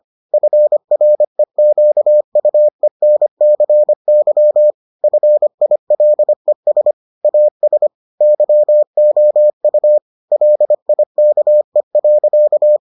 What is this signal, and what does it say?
Morse code, heard as a keyed tone, about 25 wpm, 600 Hz. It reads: FREQUENCY FILES AS YOU LIKE.